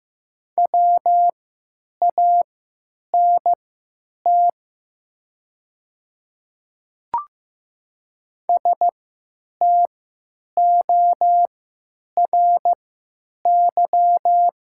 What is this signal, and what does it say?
Morse code, heard as a keyed tone, about 15 wpm, spaced as for 9 wpm, 700 Hz. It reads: WANT STORY
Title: 3x Character Spacing - Top 300 Words - 15wpm